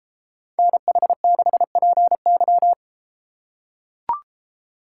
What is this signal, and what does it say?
Morse code, heard as a keyed tone, about 30 wpm, 700 Hz. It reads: NH6PY